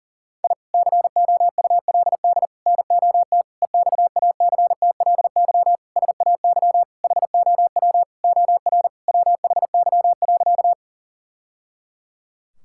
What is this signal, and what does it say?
Morse code, heard as a keyed tone, about 40 wpm, 700 Hz. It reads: I COULD NOT EXACTLY SAY HOW OR WHY.
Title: Sentences from Top 1000 Words - Review Part 6 - 40wpm